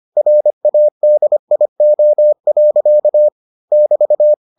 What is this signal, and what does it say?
Morse code, heard as a keyed tone, about 25 wpm, 600 Hz. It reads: RADIO. <BT>